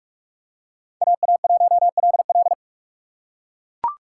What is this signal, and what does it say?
Morse code, heard as a keyed tone, about 43 wpm, 700 Hz. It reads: AA1LL
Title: USA Call Signs - 5 Characters - 4x Character Spacing - 45wpm